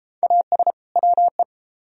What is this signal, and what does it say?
Morse code, heard as a keyed tone, about 31 wpm, 700 Hz. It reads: AS WE